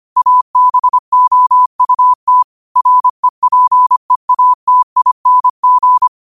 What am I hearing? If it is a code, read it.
Morse code, heard as a keyed tone, about 25 wpm, 1000 Hz. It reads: ABOUT REPEATING
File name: qotdfiles / 25wpm